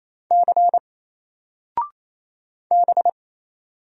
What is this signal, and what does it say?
Morse code, heard as a keyed tone, about 27 wpm, 700 Hz. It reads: C B
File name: Single-Letter-Number-28wpm